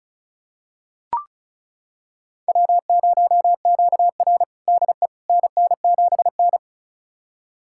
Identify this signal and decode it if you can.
Morse code, heard as a keyed tone, about 34 wpm, 700 Hz. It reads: W0QR DE NN7N
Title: USA Callsign DE Callsign - 4 Characters - 35wpm